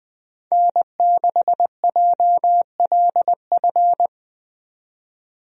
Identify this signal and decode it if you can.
Morse code, heard as a keyed tone, about 21 wpm, 700 Hz. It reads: N6JLF
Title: USA Call Signs - 5 Characters - 20wpm